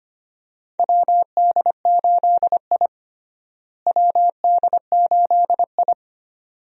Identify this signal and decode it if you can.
Morse code, heard as a keyed tone, about 25 wpm, 700 Hz. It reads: WD8I WD8I
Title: USA Call Signs - 4 Characters - Repeated 2x - 25wpm